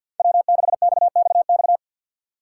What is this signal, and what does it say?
Morse code, heard as a keyed tone, about 53 wpm, 690 Hz. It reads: W6XXX
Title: USA Call Signs - 6x Character Spacing - 50wpm